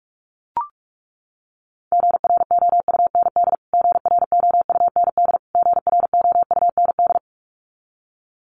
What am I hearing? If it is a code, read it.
Morse code, heard as a keyed tone, about 44 wpm, 700 Hz. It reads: GROUND GROUND GROUND
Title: Top 500 Words Repeated 45wpm